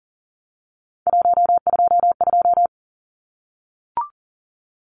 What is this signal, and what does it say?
Morse code, heard as a keyed tone, about 40 wpm, 700 Hz. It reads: 122
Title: Numbers - 3-Digit 40wpm